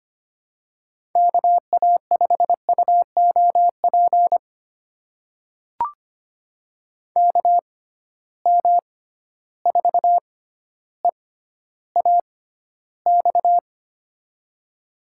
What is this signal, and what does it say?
Morse code, heard as a keyed tone, about 25 wpm, spaced as for 13 wpm, 700 Hz. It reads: KA5UOP KM4EAX
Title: USA Call Signs - 6 Characters - 6x Character Spacing - 25wpm